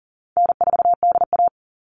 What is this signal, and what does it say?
Morse code, heard as a keyed tone, about 40 wpm, 700 Hz. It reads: N4DA
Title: NA SOTA Chaser Callsigns 40wpm